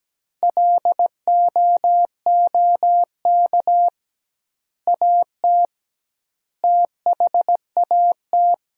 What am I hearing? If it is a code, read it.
Morse code, heard as a keyed tone, about 17 wpm, 700 Hz. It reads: LOOK AT THAT
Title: 2x Word-Spacing - Sentences from Top 500 Words - 17wpm